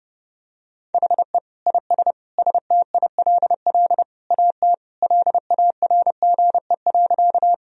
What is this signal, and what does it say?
Morse code, heard as a keyed tone, about 30 wpm, 700 Hz. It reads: HE IS STILL AT LARGE.
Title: Sentences from Top 1000 Words - Review Part 1 - 30wpm